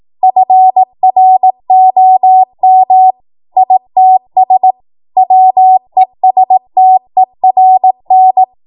Morse code, heard as a keyed tone, about 18 wpm, 750 Hz. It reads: FROM ITS WESTERN
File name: Morse / 18WPM